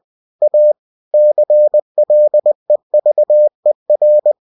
Morse code, heard as a keyed tone, about 20 wpm, 600 Hz. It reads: A CLEVER
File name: cw-books / winnie-the-pooh / winnie-20wpm-page-0091